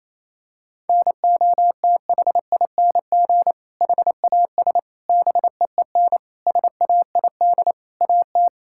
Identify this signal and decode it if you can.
Morse code, heard as a keyed tone, about 28 wpm, 700 Hz. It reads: NOTHING HAS BEEN SAID AT
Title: Sentences from Top 1000 Words - Review Part 5 - 28wpm